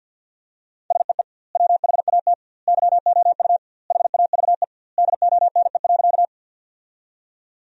Morse code, heard as a keyed tone, about 46 wpm, 700 Hz. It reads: SEE WHAT YOU HAVE DONE.
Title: 2x Word-Spacing - Sentences from Top 500 Words - 50wpm